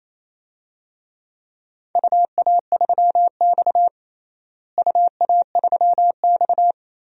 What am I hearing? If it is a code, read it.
Morse code, heard as a keyed tone, about 28 wpm, 700 Hz. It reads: UA3X UA3X
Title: DXCC-100 Call Signs - 4 Characters - Repeated 2x - 28wpm